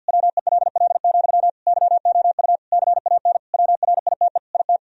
Morse code, heard as a keyed tone, about 50 wpm, 700 Hz. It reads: WELL, YOU CAN WRITE IT
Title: Sentences from Top 1000 Words - Review Part 4 - 50wpm